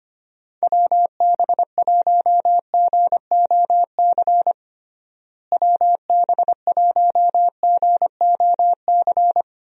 Morse code, heard as a keyed tone, about 25 wpm, 700 Hz. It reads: WB1GOC WB1GOC
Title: USA Call Signs - 6 Characters - Repeated 3x - 25wpm